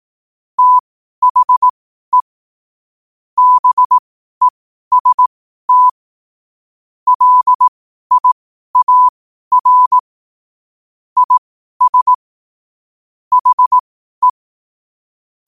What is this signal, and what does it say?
Morse code, heard as a keyed tone, about 18 wpm, spaced as for 12 wpm, 1000 Hz. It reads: THE BEST LIAR IS HE
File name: qotdfiles / 12wpm